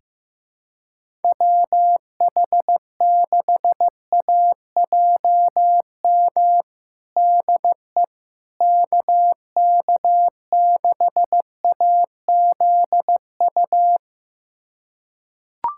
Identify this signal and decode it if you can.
Morse code, heard as a keyed tone, about 15 wpm, 700 Hz. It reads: WH6AJM DE KK6AZU
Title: USA Callsign DE Callsign 15wpm